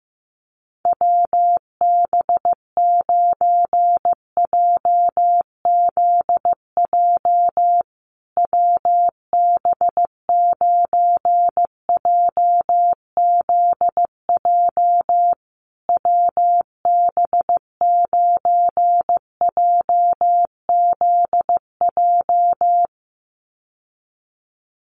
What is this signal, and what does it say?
Morse code, heard as a keyed tone, about 15 wpm, 700 Hz. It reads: WB9JZJ WB9JZJ WB9JZJ
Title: USA Call Signs Repeated 3x 15wpm